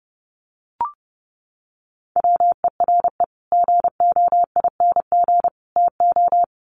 Morse code, heard as a keyed tone, about 28 wpm, 700 Hz. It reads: WERE GOING TO
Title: Sets of 3 Words 30wpm